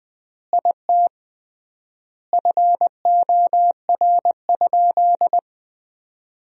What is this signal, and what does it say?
Morse code, heard as a keyed tone, about 21 wpm, 700 Hz. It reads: IT FOR?
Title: 3x Word-Spacing - Sentences from Top 100 Words - 20wpm